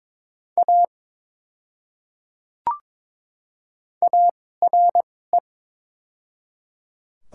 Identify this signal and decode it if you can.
Morse code, heard as a keyed tone, about 21 wpm, spaced as for 16 wpm, 690 Hz. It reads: A ARE
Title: 2x Character Spacing - Top 200 Words - 22wpm